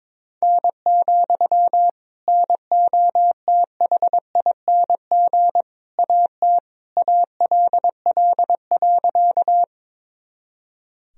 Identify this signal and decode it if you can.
Morse code, heard as a keyed tone, about 21 wpm, 700 Hz. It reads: N, NOTHING AT ALL.
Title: Sentences from Top 500 Words and Abbreviations 22wpm